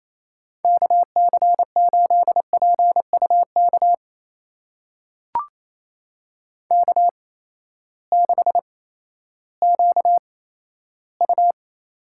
Morse code, heard as a keyed tone, about 28 wpm, spaced as for 14 wpm, 700 Hz. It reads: KC8PUK K6QU
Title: USA Call Signs - 8x Character Spacing - 28wpm